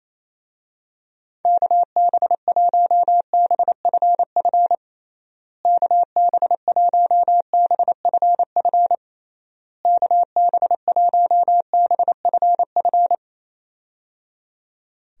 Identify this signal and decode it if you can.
Morse code, heard as a keyed tone, about 27 wpm, 700 Hz. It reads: KB1BFF KB1BFF KB1BFF
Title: USA Call Signs - 6 Characters - Repeated 3x - 28wpm